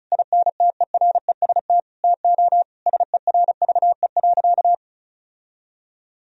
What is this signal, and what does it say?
Morse code, heard as a keyed tone, about 36 wpm, 700 Hz. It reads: INTEREST TO SERVE.
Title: Sentences from Top 1000 Words - Review Part 4 - 35wpm